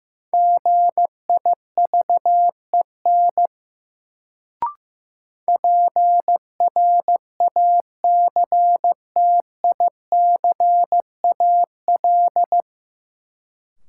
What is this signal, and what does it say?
Morse code, heard as a keyed tone, about 15 wpm, 700 Hz. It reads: GIVEN PRACTICAL
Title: Sentences from Top 1400 Words - New Word Drill - 15wpm